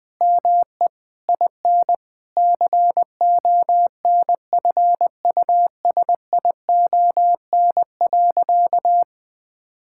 Morse code, heard as a keyed tone, about 20 wpm, 700 Hz. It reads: ME IN CONFUSION.
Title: Sentences from Top 1600 Words 20wpm